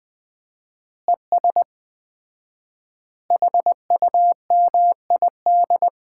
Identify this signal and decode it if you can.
Morse code, heard as a keyed tone, about 20 wpm, 700 Hz. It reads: ES HUMID
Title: Common QSO Phrases - 4x Word-Spacing - 20wpm